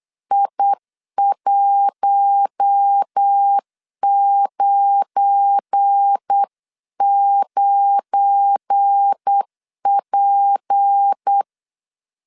Morse code, heard as a keyed tone, about 8 wpm, 800 Hz. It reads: I199P
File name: LEZIONI / 7 GF9Z / 7 GF9 8wpm 3545